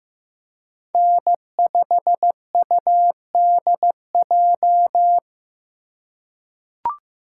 Morse code, heard as a keyed tone, about 15 wpm, 700 Hz. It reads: N5UDJ
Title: USA Call Signs - 5 Characters - 3x Character Spacing - 15wpm